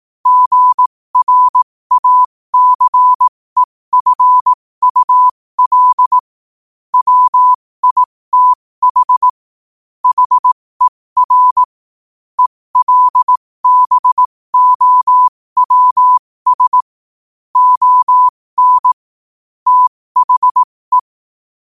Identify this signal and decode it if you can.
Morse code, heard as a keyed tone, about 18 wpm, spaced as for 15 wpm, 1000 Hz. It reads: GRACEFUL WITH HER ELBOWS ON THE
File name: qotdfiles / 15wpm